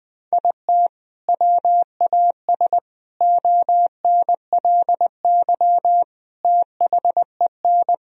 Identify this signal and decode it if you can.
Morse code, heard as a keyed tone, about 21 wpm, 700 Hz. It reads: IT WAS ONLY THEN
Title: Sentences from Top 1000 Words - Review Part 3 - 20wpm